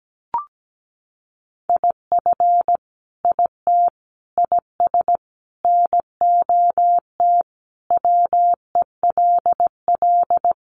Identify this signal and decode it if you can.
Morse code, heard as a keyed tone, about 17 wpm, 700 Hz. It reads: IF IT IS NOT WELL
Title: Sentences from Top 1200 Words 17wpm